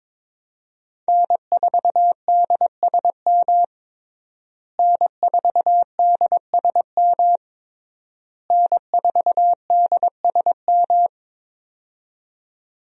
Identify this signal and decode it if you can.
Morse code, heard as a keyed tone, about 22 wpm, 700 Hz. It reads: N4DSM N4DSM N4DSM
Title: USA Call Signs - 5 Characters - Repeated 3x - 22wpm